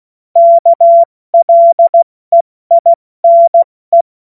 Morse code, heard as a keyed tone, about 16 wpm, 680 Hz. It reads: KLEINE
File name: Morsekurs / Witze / Witz 28.November 2023 70 F80